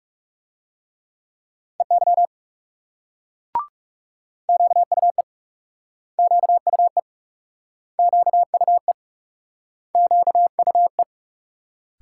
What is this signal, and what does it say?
Morse code, heard as a keyed tone, about 38 wpm, 700 Hz. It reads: EY QUE QUE QUE QUE